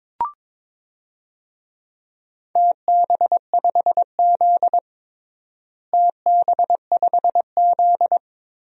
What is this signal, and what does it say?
Morse code, heard as a keyed tone, about 22 wpm, 700 Hz. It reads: TB5Z TB5Z
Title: DXCC-100 Call Signs - 4 Characters - Repeated 2x - 22wpm